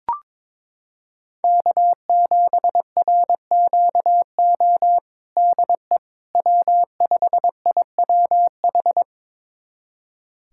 Morse code, heard as a keyed tone, about 22 wpm, 700 Hz. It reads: K7RQO DE W5IWH
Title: USA Callsign DE Callsign - 5 Characters - 22wpm